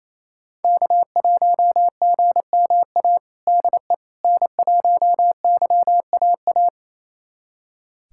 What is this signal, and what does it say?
Morse code, heard as a keyed tone, about 28 wpm, 700 Hz. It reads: K1GMA DE N1YAA
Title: USA Callsign DE Callsign - 5 Characters - 28wpm